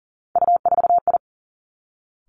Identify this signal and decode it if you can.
Morse code, heard as a keyed tone, about 39 wpm, 700 Hz. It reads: U4I
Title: Three Challenging Letter-Number Combinations 40wpm